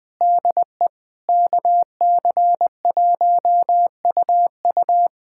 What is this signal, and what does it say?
Morse code, heard as a keyed tone, about 20 wpm, 700 Hz. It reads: DE KC1UU
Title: USA Callsign DE Callsign - 5 Characters - 20wpm